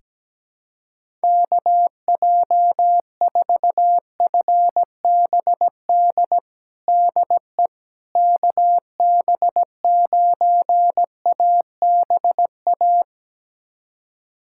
Morse code, heard as a keyed tone, about 17 wpm, 700 Hz. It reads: KJ4FBD DE KB9ABA
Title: USA Callsign DE Callsign - 6 Characters - 17wpm